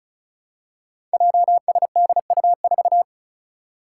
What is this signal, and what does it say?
Morse code, heard as a keyed tone, about 36 wpm, 700 Hz. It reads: JSDU4
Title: Five Challenging Letter-Number Combinations 35wpm